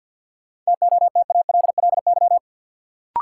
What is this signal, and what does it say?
Morse code, heard as a keyed tone, about 52 wpm, 700 Hz. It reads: TOTALLY
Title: Sentences from Top 1600 Words - New Word Drill - 50wpm